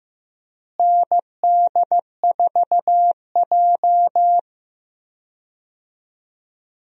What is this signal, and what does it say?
Morse code, heard as a keyed tone, about 15 wpm, 700 Hz. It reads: ND4J
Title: USA Call Signs - 4 Characters - Repeated 3x - 15wpm